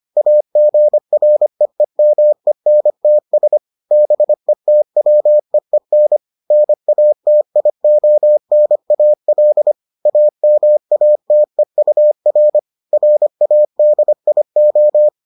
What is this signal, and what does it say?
Morse code, heard as a keyed tone, about 25 wpm, 600 Hz. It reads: AGREEMENTS BETWEEN NATIONAL AMATEUR RADIO